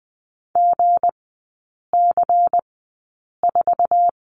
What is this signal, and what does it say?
Morse code, heard as a keyed tone, about 20 wpm, spaced as for 8 wpm, 700 Hz. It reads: GC4
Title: 178 Lesson 29e - Slash - Three Characters 20wpm